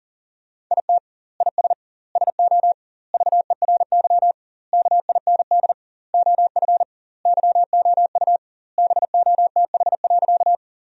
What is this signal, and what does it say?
Morse code, heard as a keyed tone, about 40 wpm, 700 Hz. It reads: IT IS SO VERY KIND OF YOU BOTH.
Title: 2x Word-Spacing - Sentences from Top 300 Words - 40wpm